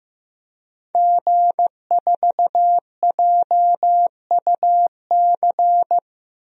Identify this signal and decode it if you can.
Morse code, heard as a keyed tone, about 15 wpm, 700 Hz. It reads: G4JUC
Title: DXCC-100 Call Signs - 5 Characters - Repeated 3x - 15wpm